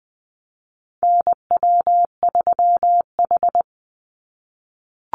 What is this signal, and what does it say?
Morse code, heard as a keyed tone, about 20 wpm, 700 Hz. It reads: NW3H
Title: USA Call Signs 20wpm